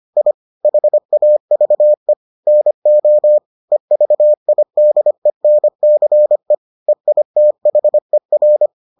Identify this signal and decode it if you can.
Morse code, heard as a keyed tone, about 25 wpm, 600 Hz. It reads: I HAVE NO EVIDENCE EITHER